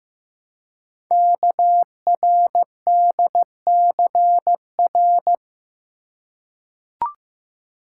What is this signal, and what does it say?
Morse code, heard as a keyed tone, about 15 wpm, 700 Hz. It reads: KRDCR
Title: Code Groups - 5 Letters - 15wpm